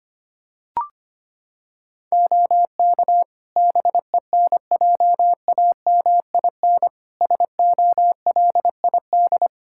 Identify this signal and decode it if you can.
Morse code, heard as a keyed tone, about 25 wpm, 700 Hz. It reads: OK BENJAMIN SOLID
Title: Practice QSO - Exchange 3 - More Info - 25wpm